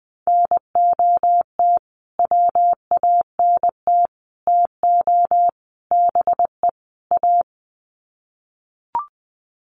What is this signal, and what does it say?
Morse code, heard as a keyed tone, about 20 wpm, 700 Hz. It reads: NOT WANT TO BE A
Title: Sets of 5 Words Encore 20wpm